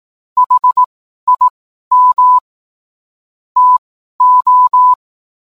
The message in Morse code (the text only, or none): HIM TO